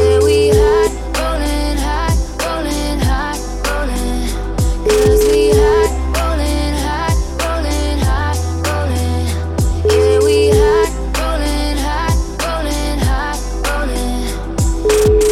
Ringback tone (Europe, 425 Hz), with noise.